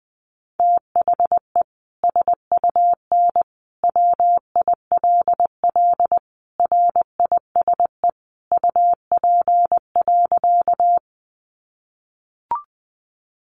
Morse code, heard as a keyed tone, about 20 wpm, 700 Hz. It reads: THE SUN WILL RISE UP.
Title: Sentences from Top 900 Words 20wpm